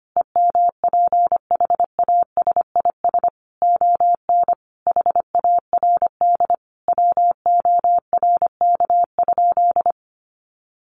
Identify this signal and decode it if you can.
Morse code, heard as a keyed tone, about 25 wpm, 700 Hz. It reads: EMPHASIS ON HARD WORK?
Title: Sentences from Top 1400 Words 25wpm